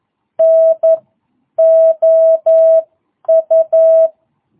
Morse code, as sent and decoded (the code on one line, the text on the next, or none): -. / --- ..-
N OU